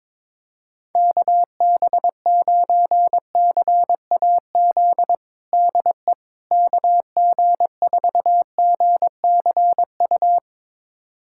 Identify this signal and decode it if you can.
Morse code, heard as a keyed tone, about 22 wpm, 700 Hz. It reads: KB9CAZ DE KG4GCU
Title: USA Callsign DE Callsign 22wpm